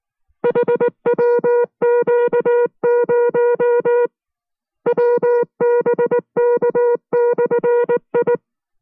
Morse code, heard as a keyed tone, about 19 wpm, 480 Hz. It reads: HWQ0 WBK/I